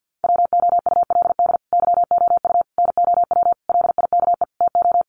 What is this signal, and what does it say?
Morse code, heard as a keyed tone, about 48 wpm, 700 Hz. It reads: WOULD YOU NOW LIKE TO
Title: Sentences from Top 200 Words 50wpm